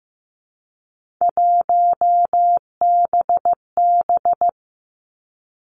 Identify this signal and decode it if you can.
Morse code, heard as a keyed tone, about 15 wpm, 700 Hz. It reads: 1BB
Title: Three Challenging Letter-Number Combinations 15wpm